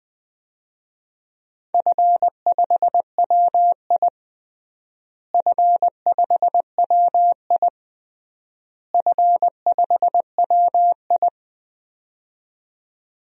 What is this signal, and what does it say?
Morse code, heard as a keyed tone, about 21 wpm, 700 Hz. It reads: F5WI F5WI F5WI
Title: DXCC-100 Call Signs - 4 Characters - Repeated 3x - 20wpm